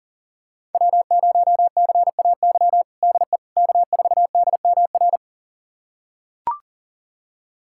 Morse code, heard as a keyed tone, about 40 wpm, 700 Hz. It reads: W0CAY DE K4DMR
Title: USA Callsign DE Callsign - 5 Characters - 40wpm